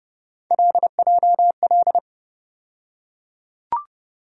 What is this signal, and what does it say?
Morse code, heard as a keyed tone, about 32 wpm, 700 Hz. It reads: LJL
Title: USA Call Sign Postfixes - 3 Letters 30wpm